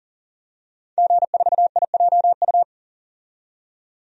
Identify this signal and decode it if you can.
Morse code, heard as a keyed tone, about 40 wpm, 700 Hz. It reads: G4IJU
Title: DXCC-100 Call Signs - 5 Characters - 40wpm